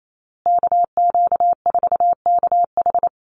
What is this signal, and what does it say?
Morse code, heard as a keyed tone, about 28 wpm, 700 Hz. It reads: KQ4KH
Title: USA Call Signs Repeated 2x 28wpm